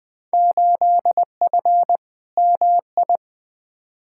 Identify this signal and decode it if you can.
Morse code, heard as a keyed tone, about 20 wpm, 700 Hz. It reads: 8F MI
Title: Field Day Exchanges Encore - 20wpm